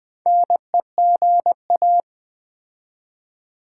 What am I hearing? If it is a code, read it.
Morse code, heard as a keyed tone, about 20 wpm, 700 Hz. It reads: NEGA
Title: Build A Word - 2 Letters At A Time - 8-Letter Words - 20wpm